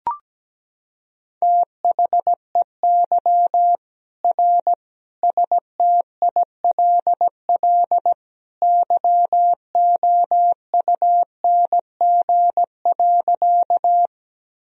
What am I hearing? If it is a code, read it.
Morse code, heard as a keyed tone, about 17 wpm, 700 Hz. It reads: THEY R STILL YOUNG.
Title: Sentences from Top 400 Words and Abbreviations 17wpm